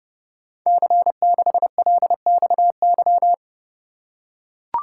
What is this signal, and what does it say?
Morse code, heard as a keyed tone, about 28 wpm, 700 Hz. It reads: C6LXY
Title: Code Groups - 5 Letter:Numbers - 30wpm